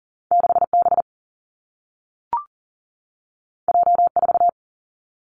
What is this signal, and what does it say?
Morse code, heard as a keyed tone, about 41 wpm, 700 Hz. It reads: 6B J4
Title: Two Challenging Letter-Number Combinations 40wpm